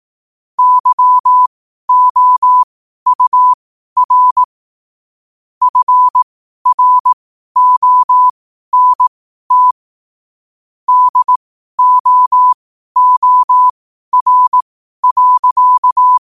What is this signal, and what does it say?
Morse code, heard as a keyed tone, about 18 wpm, spaced as for 12 wpm, 1000 Hz. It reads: YOUR FRONT DOOR.
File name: qotdfiles / 12wpm